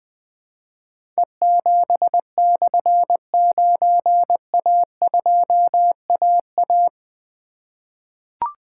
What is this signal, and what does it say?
Morse code, heard as a keyed tone, about 20 wpm, 700 Hz. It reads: E7/9A2AA
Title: Difficult Call Signs 20wpm